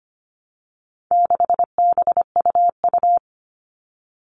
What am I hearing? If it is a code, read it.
Morse code, heard as a keyed tone, about 25 wpm, 700 Hz. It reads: 6BUU
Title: All Cut Numbers - 4-Digit 25wpm